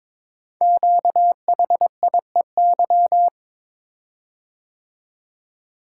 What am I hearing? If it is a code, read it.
Morse code, heard as a keyed tone, about 22 wpm, 700 Hz. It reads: QHIEY
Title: Code Groups - 5 Letters - 22wpm